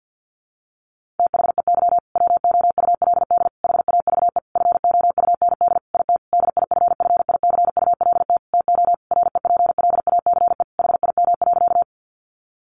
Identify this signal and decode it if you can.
Morse code, heard as a keyed tone, about 50 wpm, 700 Hz. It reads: THEY WOULD HAVE FOUND IT DIFFICULT TO REPLACE HIM.
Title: Sentences from Top 1400 Words 50wpm